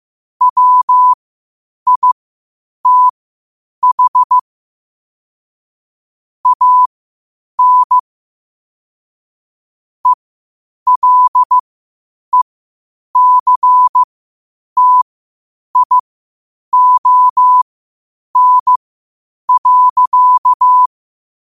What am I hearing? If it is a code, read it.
Morse code, heard as a keyed tone, about 15 wpm, spaced as for 8 wpm, 1000 Hz. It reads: WITH AN ELECTION.
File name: qotdfiles / 8wpm